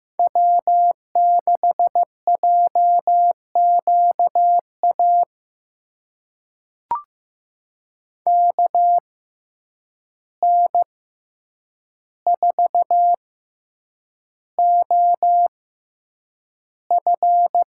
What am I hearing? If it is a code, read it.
Morse code, heard as a keyed tone, about 15 wpm, spaced as for 8 wpm, 700 Hz. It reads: W6JQA KN4OF
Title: USA Call Signs - 6x Character Spacing - 15wpm